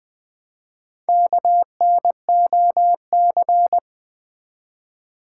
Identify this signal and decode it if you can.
Morse code, heard as a keyed tone, about 20 wpm, 700 Hz. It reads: KNOC
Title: Build A Word - 1 Letter At A Time - 3 to 5-Letter Words - 20wpm